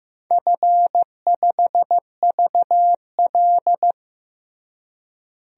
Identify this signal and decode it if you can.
Morse code, heard as a keyed tone, about 14 wpm, 700 Hz. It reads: F5VL
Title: DXCC-100 Call Signs - 4 Characters - Repeated 3x - 15wpm